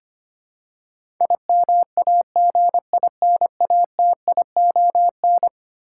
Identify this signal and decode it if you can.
Morse code, heard as a keyed tone, about 25 wpm, 700 Hz. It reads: IMAGINATION
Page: morse-code-ninja-practice-25wpm